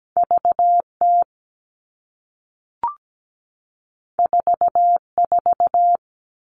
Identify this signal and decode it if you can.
Morse code, heard as a keyed tone, about 17 wpm, 700 Hz. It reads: VT 44
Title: All Cut Numbers - 2-Digit 17wpm